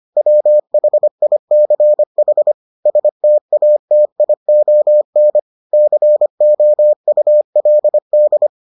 Morse code, heard as a keyed tone, about 25 wpm, 600 Hz. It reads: WHICH STATION COULD